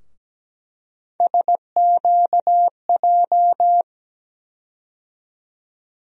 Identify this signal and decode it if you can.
Morse code, heard as a keyed tone, about 17 wpm, 700 Hz. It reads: SQJ
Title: USA Call Sign Postfixes - 3 Letters 17wpm